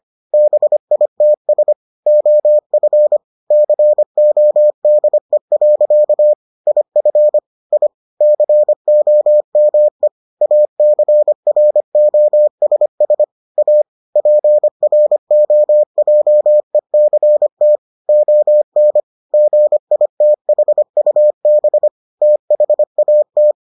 Morse code, heard as a keyed tone, about 25 wpm, 600 Hz. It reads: BITS OF CODE. IF I COME ACROSS A PROJECT ON GITHUB THAT